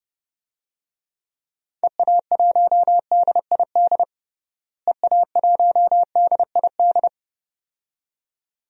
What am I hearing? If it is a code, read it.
Morse code, heard as a keyed tone, about 30 wpm, 700 Hz. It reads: EA1DID EA1DID
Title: DXCC-100 Call Signs - 6 Characters - Repeated 2x - 30wpm